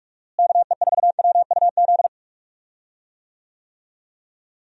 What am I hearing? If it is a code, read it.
Morse code, heard as a keyed tone, about 46 wpm, 700 Hz. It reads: KE4WUZ
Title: USA Call Signs - 6 Characters - Repeated 3x - 45wpm